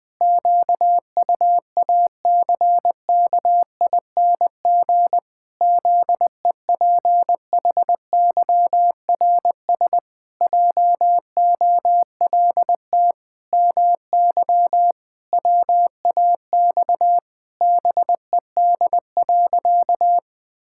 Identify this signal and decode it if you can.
Morse code, heard as a keyed tone, about 20 wpm, 700 Hz. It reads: QUACKING ZEPHYRS JOLT MY WAX BED.
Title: Pangram Challenge v2 - 20wpm